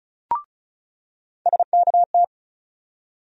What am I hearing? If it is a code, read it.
Morse code, heard as a keyed tone, about 36 wpm, 700 Hz. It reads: SKT
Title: USA Call Sign Postfixes - 3 Letters 35wpm